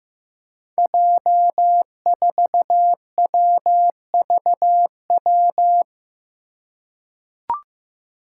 Morse code, heard as a keyed tone, about 15 wpm, 700 Hz. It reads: J4WVW
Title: Five Challenging Letter-Number Combinations 15wpm